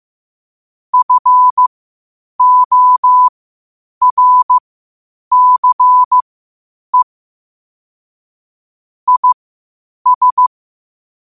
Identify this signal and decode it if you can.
Morse code, heard as a keyed tone, about 15 wpm, spaced as for 8 wpm, 1000 Hz. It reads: FORCE IS
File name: 8wpm